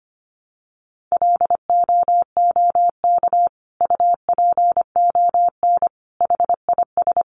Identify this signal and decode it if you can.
Morse code, heard as a keyed tone, about 25 wpm, 700 Hz. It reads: LOOK UPON HIS
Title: Sentences from Top 1100 Words 25wpm